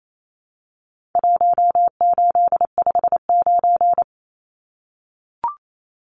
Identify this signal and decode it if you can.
Morse code, heard as a keyed tone, about 28 wpm, 700 Hz. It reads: 1859
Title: Numbers - 4-Digit 28wpm